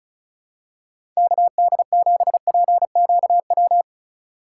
Morse code, heard as a keyed tone, about 34 wpm, 700 Hz. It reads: KD7PQW
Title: USA Call Signs - 6x Character Spacing - 35wpm